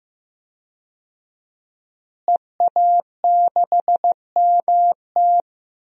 Morse code, heard as a keyed tone, about 15 wpm, 700 Hz. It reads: EA6MT
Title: DXCC-100 Call Signs - 5 Characters - Repeated 2x - 15wpm